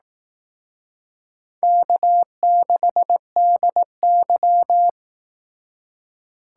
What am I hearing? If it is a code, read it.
Morse code, heard as a keyed tone, about 18 wpm, 700 Hz. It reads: K6DY
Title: Speed Racing - USA 1x2 Call Signs 15wpm